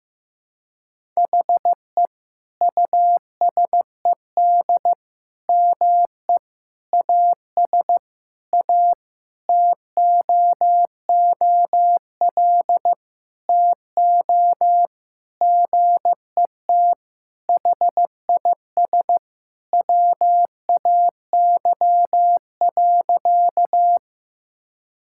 Morse code, heard as a keyed tone, about 15 wpm, 700 Hz. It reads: HE USED ME AS A TOOL TO GET HIS WAY.
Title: Sentences from Top 1000 Words - Review Part 6 - 15wpm